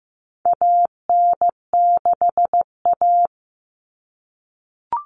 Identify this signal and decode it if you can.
Morse code, heard as a keyed tone, about 15 wpm, 700 Hz. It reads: AN6A
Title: Common Cut Numbers - 4-Digit 15wpm